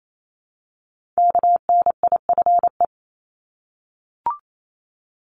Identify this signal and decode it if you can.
Morse code, heard as a keyed tone, about 28 wpm, 700 Hz. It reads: KNIFE
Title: Multi-letter Phonemes Repeated 3x 28wpm